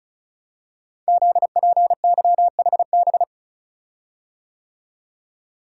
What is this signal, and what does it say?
Morse code, heard as a keyed tone, about 36 wpm, 690 Hz. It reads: ZPYHB
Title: Code Groups - 5 Letter:Numbers - 35wpm